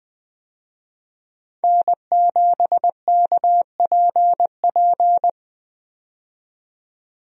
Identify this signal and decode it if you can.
Morse code, heard as a keyed tone, about 21 wpm, 700 Hz. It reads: N7KPP
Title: USA Call Signs - 5 Characters - 20wpm